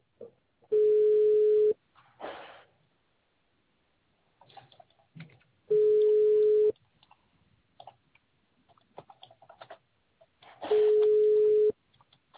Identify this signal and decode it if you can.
Ringback tone (Europe, 425 Hz).